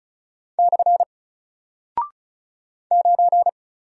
Morse code, heard as a keyed tone, about 36 wpm, 700 Hz. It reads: / 9
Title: Single Character (Letter:Number:Punctuation:Prosign) Alt 35wpm